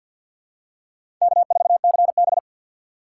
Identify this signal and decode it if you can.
Morse code, heard as a keyed tone, about 52 wpm, 700 Hz. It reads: K4CB
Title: USA Call Signs - 4 Characters - 6x Character Spacing - 50wpm